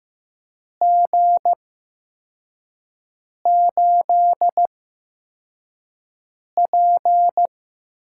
Morse code, heard as a keyed tone, about 15 wpm, spaced as for 4 wpm, 700 Hz. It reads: G8P
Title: USA Call Signs - 4 Characters - 8x Character Spacing - 15wpm